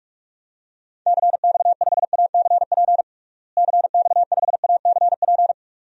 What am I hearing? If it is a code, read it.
Morse code, heard as a keyed tone, about 46 wpm, 700 Hz. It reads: CX5ACP CX5ACP
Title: DXCC-100 Call Signs - 6 Characters - Repeated 2x - 45wpm